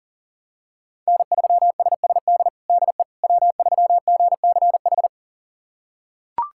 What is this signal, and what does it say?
Morse code, heard as a keyed tone, about 40 wpm, 700 Hz. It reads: N3SSD DE W3GCH
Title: USA Callsign DE Callsign - 5 Characters - 40wpm